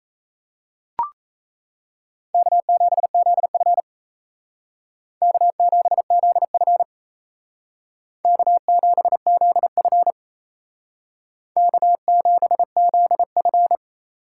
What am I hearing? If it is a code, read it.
Morse code, heard as a keyed tone, about 35 wpm, 700 Hz. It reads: K7ZF K7ZF K7ZF K7ZF